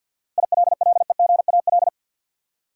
Morse code, heard as a keyed tone, about 48 wpm, 700 Hz. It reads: ILLEGAL
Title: Seven Letter Words - 50wpm